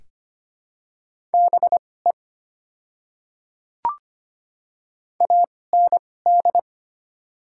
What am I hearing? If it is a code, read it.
Morse code, heard as a keyed tone, about 25 wpm, spaced as for 18 wpm, 700 Hz. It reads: BE AND
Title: 2x Character Spacing - Top 100 Words - 25wpm